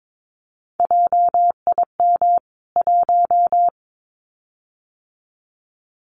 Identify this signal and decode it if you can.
Morse code, heard as a keyed tone, about 22 wpm, 700 Hz. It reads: JIM 1
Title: CWT Exchange 22wpm